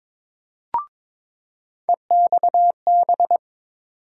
Morse code, heard as a keyed tone, about 22 wpm, 700 Hz. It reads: EXB
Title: USA Call Sign Postfixes - 3 Letters 22wpm